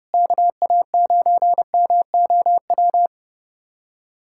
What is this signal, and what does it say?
Morse code, heard as a keyed tone, about 32 wpm, 700 Hz. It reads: KA9MOW
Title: USA Callsign DE Callsign - 6 Characters - 30wpm